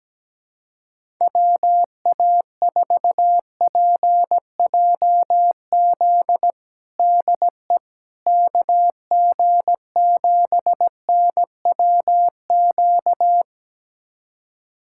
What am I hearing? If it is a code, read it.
Morse code, heard as a keyed tone, about 17 wpm, 700 Hz. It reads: WA4PJZ DE KG7NWQ
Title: USA Callsign DE Callsign 17wpm